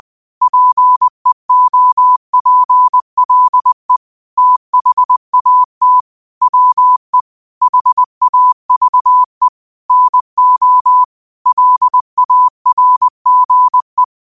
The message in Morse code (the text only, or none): PEOPLE THAT WE HAVE NO LARGE